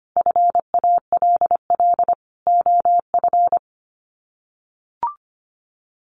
Morse code, heard as a keyed tone, about 24 wpm, 700 Hz. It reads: FALL OF
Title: Sets of 5 Words Encore2 25wpm